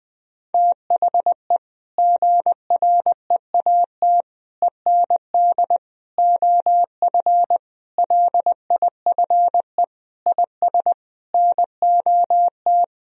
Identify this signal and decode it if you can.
Morse code, heard as a keyed tone, about 21 wpm, 700 Hz. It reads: THE GREAT END OF LIFE IS NOT
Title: Sentences from Top 1000 Words - Review Part 4 - 20wpm